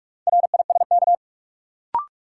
Morse code, heard as a keyed tone, about 46 wpm, 700 Hz. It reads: RISK
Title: Sentences from Top 600 Words - New Word Drill - 45wpm